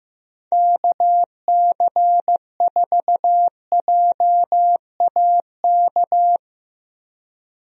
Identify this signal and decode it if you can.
Morse code, heard as a keyed tone, about 15 wpm, 700 Hz. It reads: KC4JAK
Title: USA Call Signs - 6 Characters - 15wpm